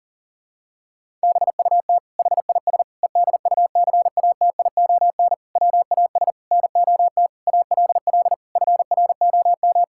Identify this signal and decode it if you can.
Morse code, heard as a keyed tone, about 37 wpm, 700 Hz. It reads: BUT HIS EDUCATION WAS NOT ALL FROM
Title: Sentences from Top 1000 Words - Review Part 6 - 40wpm